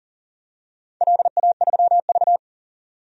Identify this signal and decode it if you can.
Morse code, heard as a keyed tone, about 40 wpm, 700 Hz. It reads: LA3V
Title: DXCC-100 Call Signs - 4 Characters - 40wpm